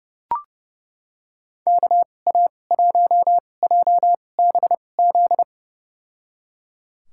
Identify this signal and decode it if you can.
Morse code, heard as a keyed tone, about 30 wpm, spaced as for 22 wpm, 700 Hz. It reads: KA1JBZ
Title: USA Call Signs - 2x Character Spacing - 30wpm